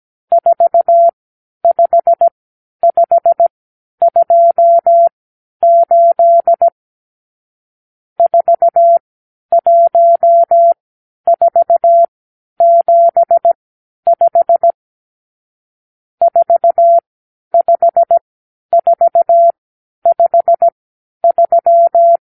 Morse code, heard as a keyed tone, about 17 wpm, spaced as for 10 wpm, 680 Hz. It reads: 45528 41475 45453
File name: k7qo / Lesson 54 (9)